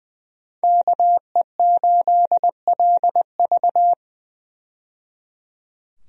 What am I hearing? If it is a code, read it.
Morse code, heard as a keyed tone, about 21 wpm, 700 Hz. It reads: KE8LV
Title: USA Call Signs - 5 Characters - 20wpm